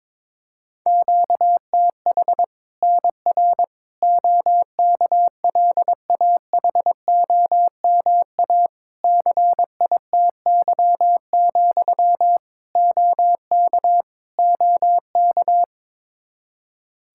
Morse code, heard as a keyed tone, about 22 wpm, 700 Hz. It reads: QTH NR OKLAHOMA CITY, OK OK
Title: Practice QSO - Exchange 1 - Beginning - 22wpm